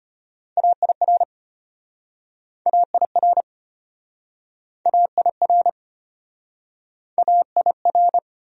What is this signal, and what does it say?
Morse code, heard as a keyed tone, about 34 wpm, 700 Hz. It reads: AIR AIR AIR AIR